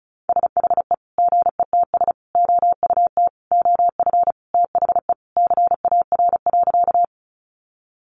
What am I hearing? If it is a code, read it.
Morse code, heard as a keyed tone, about 34 wpm, 700 Hz. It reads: SHE GETS OUT OF THE CAR.
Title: Easy Phrases 35wpm